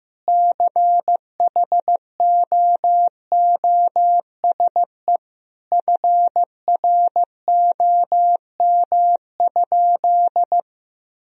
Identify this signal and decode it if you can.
Morse code, heard as a keyed tone, about 15 wpm, 700 Hz. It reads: CHOOSE FROM?
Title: Sentences from Top 1000 Words - Review Part 4 - 15wpm